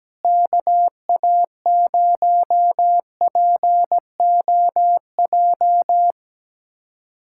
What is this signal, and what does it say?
Morse code, heard as a keyed tone, about 17 wpm, 700 Hz. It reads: KA0POJ
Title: USA Callsign DE Callsign - 6 Characters - 17wpm